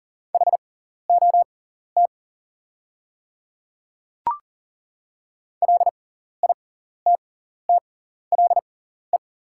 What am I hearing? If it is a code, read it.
Morse code, heard as a keyed tone, about 40 wpm, spaced as for 14 wpm, 700 Hz. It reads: HOT LITTLE
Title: 6x Character Spacing - Top 500 Words - 40wpm